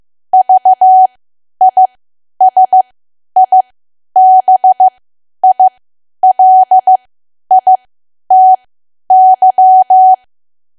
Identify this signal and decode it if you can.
Morse code, heard as a keyed tone, about 15 wpm, spaced as for 10 wpm, 750 Hz. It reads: VISIBILITY
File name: cwa / W1AW-Level2 / 10WPM